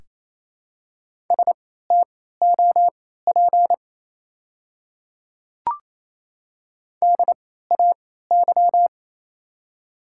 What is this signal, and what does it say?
Morse code, heard as a keyed tone, about 28 wpm, spaced as for 16 wpm, 700 Hz. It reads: STOP DAY
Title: 3x Character Spacing - Top 300 Words - 28wpm